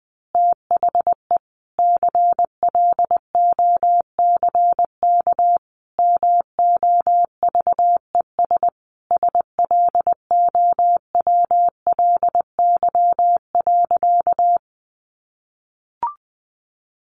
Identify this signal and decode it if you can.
Morse code, heard as a keyed tone, about 20 wpm, 700 Hz. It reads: THE CLOCK MOVES SLOWLY.
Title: Easy Phrases 20wpm